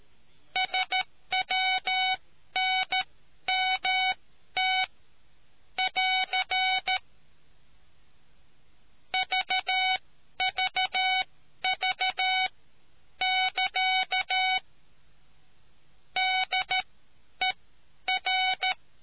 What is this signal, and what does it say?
Morse code, heard as a keyed tone, about 13 wpm, spaced as for 10 wpm, 750 Hz. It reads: SWNMT <AR> VVV<KA> DER